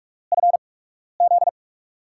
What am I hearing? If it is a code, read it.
Morse code, heard as a keyed tone, about 44 wpm, spaced as for 12 wpm, 690 Hz. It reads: FZ
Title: USA Call Signs - 4 Characters - 8x Character Spacing - 45wpm